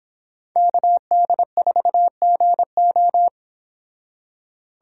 Morse code, heard as a keyed tone, about 30 wpm, 700 Hz. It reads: KD4GO